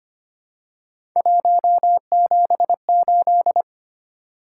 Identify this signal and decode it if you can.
Morse code, heard as a keyed tone, about 25 wpm, 700 Hz. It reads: 178
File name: Three Letter-Number Combinations - 25wpm